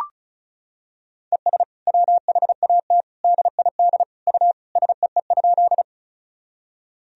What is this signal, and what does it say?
Morse code, heard as a keyed tone, about 34 wpm, 700 Hz. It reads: ES WHAT DID U SEE?
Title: Sentences from Top 100 Words and Abbreviations 35wpm